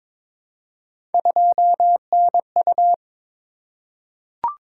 Morse code, heard as a keyed tone, about 22 wpm, 700 Hz. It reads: 2NU
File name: practice / 22wpm / Three Letter-Number Combinations - 22wpm